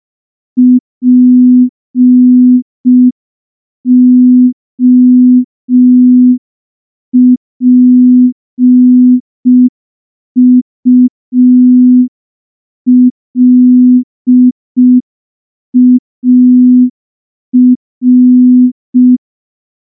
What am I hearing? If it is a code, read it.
Morse code, heard as a keyed tone, about 5 wpm, 250 Hz. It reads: POPULAR
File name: morse / morse-training-05WPM